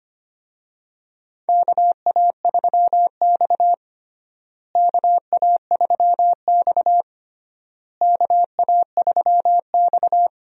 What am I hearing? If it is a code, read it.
Morse code, heard as a keyed tone, about 25 wpm, 700 Hz. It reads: KA3X KA3X KA3X
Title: USA Call Signs - 4 Characters - Repeated 3x - 25wpm